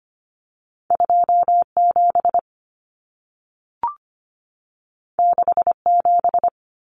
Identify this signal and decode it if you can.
Morse code, heard as a keyed tone, about 25 wpm, 700 Hz. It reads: 27 67
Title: Common Cut Numbers - 2-Digit 25wpm